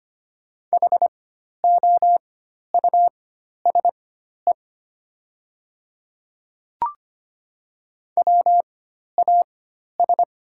Morse code, heard as a keyed tone, about 25 wpm, spaced as for 12 wpm, 700 Hz. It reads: HOUSE WAS
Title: 4x Character Spacing - Top 200 Words - 25wpm